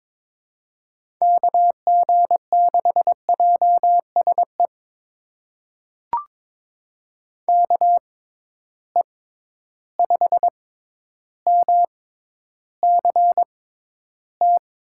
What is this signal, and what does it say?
Morse code, heard as a keyed tone, about 22 wpm, spaced as for 11 wpm, 700 Hz. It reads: KG6JSE KE5MCT
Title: USA Call Signs - 6 Characters - 6x Character Spacing - 22wpm